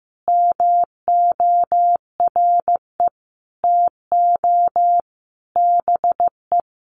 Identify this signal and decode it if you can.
Morse code, heard as a keyed tone, about 15 wpm, 700 Hz. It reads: MORE TO BE
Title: Sentences from Top 500 Words 15wpm